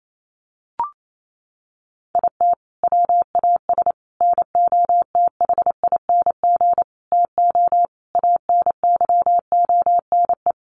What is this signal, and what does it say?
Morse code, heard as a keyed tone, about 28 wpm, 700 Hz. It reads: IT WAS NOTHING TO ANYONE
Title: Sentences from Top 1300 Words 28wpm